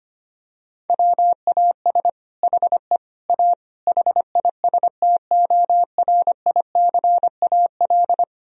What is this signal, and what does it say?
Morse code, heard as a keyed tone, about 25 wpm, 700 Hz. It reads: WAS HE A HISTORICAL
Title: Sentences from Top 1000 Words - Review Part 4 - 25wpm